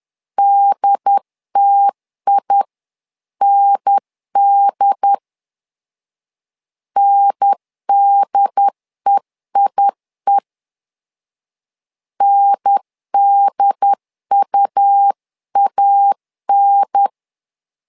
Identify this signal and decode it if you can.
Morse code, heard as a keyed tone, about 10 wpm, 800 Hz. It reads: DTIND NDEIE NDUAN
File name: LEZIONI / 2 UHDN / 2 UHDN 10wpm 1904